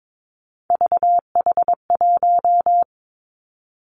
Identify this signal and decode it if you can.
Morse code, heard as a keyed tone, about 22 wpm, 700 Hz. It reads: VH1
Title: Three Challenging Letter-Number Combinations 22wpm